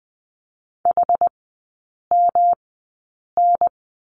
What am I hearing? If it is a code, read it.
Morse code, heard as a keyed tone, about 20 wpm, spaced as for 8 wpm, 700 Hz. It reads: HMN
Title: 121 Lesson 20e - Number 3 - Three Characters 20wpm